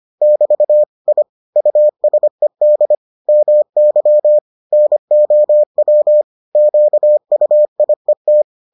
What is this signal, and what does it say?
Morse code, heard as a keyed tone, about 25 wpm, 600 Hz. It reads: <BT> I USED MY NOW QUIET